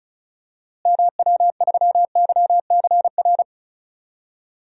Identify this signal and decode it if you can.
Morse code, heard as a keyed tone, about 34 wpm, 700 Hz. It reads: MW3YCR
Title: DXCC-100 Call Signs - 6 Characters - Repeated 3x - 35wpm